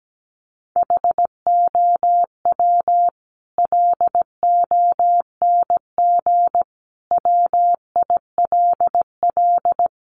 Morse code, heard as a keyed tone, about 17 wpm, 700 Hz. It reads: HOW LONG WILL
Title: Sentences from Top 1200 Words 17wpm